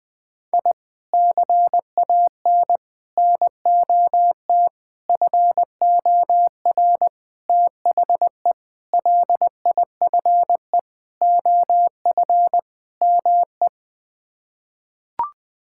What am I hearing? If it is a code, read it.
Morse code, heard as a keyed tone, about 21 wpm, 700 Hz. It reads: I CAN NOT FOR THE LIFE OF ME
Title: Sets of 8 Words - 20wpm